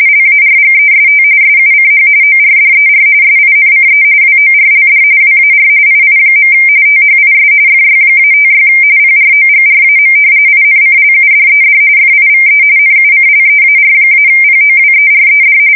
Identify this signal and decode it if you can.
RTTY, about 45.45 baud. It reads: LARGEST SUCH ORDER SINCE WORLD WAR II. THIS FY-56 PROGRAM INCLUDED FIVE NUCLEAR-POWERED
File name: fsk170 / usnavy5